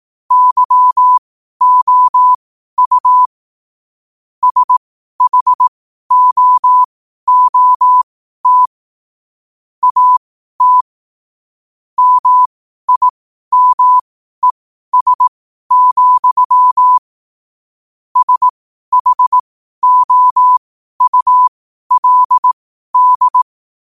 Morse code, heard as a keyed tone, about 18 wpm, spaced as for 12 wpm, 1000 Hz. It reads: YOU SHOOT AT MIMES, SHOULD
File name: qotdfiles / 12wpm